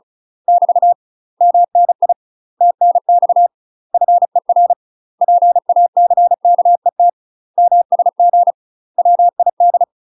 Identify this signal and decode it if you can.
Morse code, heard as a keyed tone, about 34 wpm, 700 Hz. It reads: <BT> MNI TNX FER PACKET MSG WID
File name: UK MASTER QSOs TRAINING/35WPM